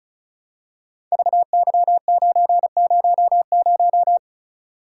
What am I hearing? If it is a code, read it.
Morse code, heard as a keyed tone, about 36 wpm, 690 Hz. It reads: VY900
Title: Code Groups - 5 Letter:Numbers - 35wpm